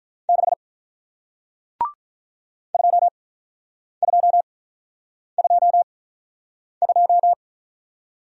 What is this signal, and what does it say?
Morse code, heard as a keyed tone, about 46 wpm, spaced as for 9 wpm, 700 Hz. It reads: 6 2222